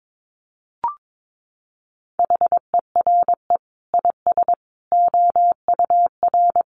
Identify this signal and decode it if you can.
Morse code, heard as a keyed tone, about 22 wpm, 700 Hz. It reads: HERE IS OUR
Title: Sentences from Top 500 Words 22wpm